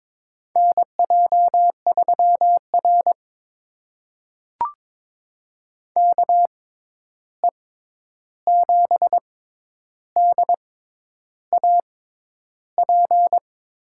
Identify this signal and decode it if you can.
Morse code, heard as a keyed tone, about 22 wpm, spaced as for 10 wpm, 700 Hz. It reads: NJ3R KE7DAP
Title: USA Call Signs - 6x Character Spacing - 22wpm